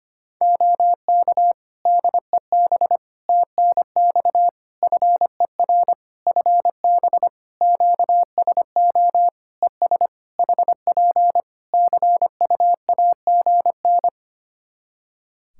Morse code, heard as a keyed tone, about 25 wpm, 700 Hz. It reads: OK DEB TNX FER FB QSO ES HP CU AGN
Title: Practice QSO - Exchange 4 - Ending - 25wpm